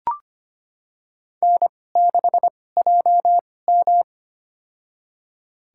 Morse code, heard as a keyed tone, about 25 wpm, spaced as for 18 wpm, 700 Hz. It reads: N6JM
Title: USA Call Signs - 4 Characters - 2x Character Spacing - 25wpm